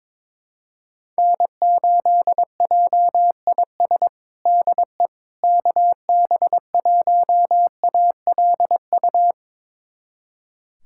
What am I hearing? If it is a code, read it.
Morse code, heard as a keyed tone, about 21 wpm, 700 Hz. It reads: N8JIS DE KB1ALU
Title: USA Callsign DE Callsign 22wpm